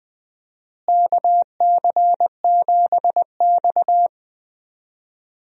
Morse code, heard as a keyed tone, about 20 wpm, 700 Hz. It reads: KC7X
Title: USA Call Signs - 4 Characters - Repeated 3x - 20wpm